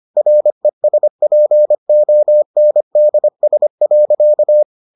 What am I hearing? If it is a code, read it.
Morse code, heard as a keyed tone, about 25 wpm, 600 Hz. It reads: RESPONDS.